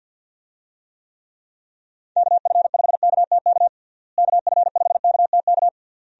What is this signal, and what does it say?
Morse code, heard as a keyed tone, about 46 wpm, 700 Hz. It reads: KF5KTK KF5KTK
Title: USA Call Signs - 6 Characters - Repeated 3x - 50wpm